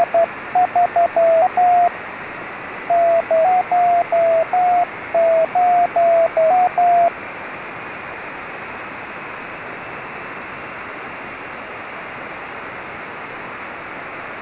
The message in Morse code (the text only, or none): none